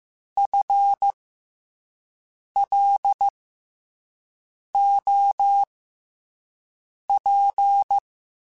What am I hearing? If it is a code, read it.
Morse code, heard as a keyed tone, about 15 wpm, spaced as for 5 wpm, 800 Hz. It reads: FLOP
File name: FL Runs / FL Runs mp3s / FL Runs @5